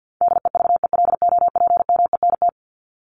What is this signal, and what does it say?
Morse code, heard as a keyed tone, about 50 wpm, 700 Hz. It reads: DEVELOPMENT
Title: Top 501-750 Words 50wpm